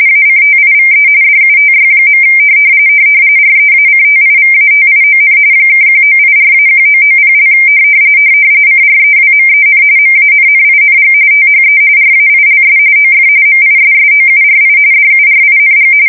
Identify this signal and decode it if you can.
RTTY, about 45.45 baud. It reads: ACTIVE DUTY. REACTIVATION (1984-1990) CREWMEN MANNING THE RAILS OF MISSOURI DURING
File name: fsk170 / USS-Missouri